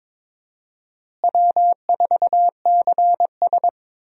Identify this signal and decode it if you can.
Morse code, heard as a keyed tone, about 22 wpm, 700 Hz. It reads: W4CS
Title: USA Call Signs - 4 Characters - Repeated 2x - 22wpm